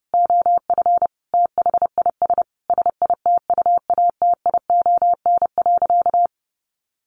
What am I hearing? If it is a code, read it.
Morse code, heard as a keyed tone, about 30 wpm, 700 Hz. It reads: OF THIS SITUATION.
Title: Sentences from Top 1200 Words 30wpm